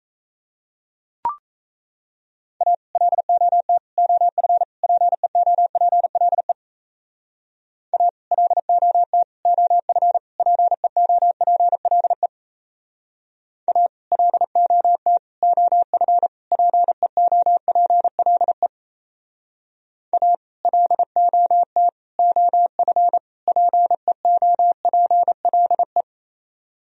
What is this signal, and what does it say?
Morse code, heard as a keyed tone, about 38 wpm, 700 Hz. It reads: A LOT OF PEOPLE A LOT OF PEOPLE A LOT OF PEOPLE A LOT OF PEOPLE